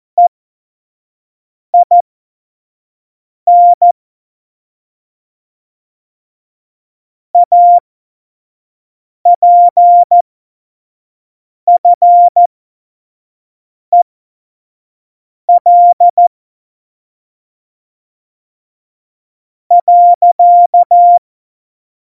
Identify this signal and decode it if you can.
Morse code, heard as a keyed tone, about 14 wpm, spaced as for 4 wpm, 700 Hz. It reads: EIN APFEL.
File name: Morsekurs / Witze / Witz 26. Dezember 2017, 25 F70